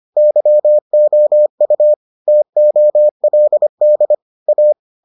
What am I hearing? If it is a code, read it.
Morse code, heard as a keyed tone, about 25 wpm, 600 Hz. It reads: YOU TOLD A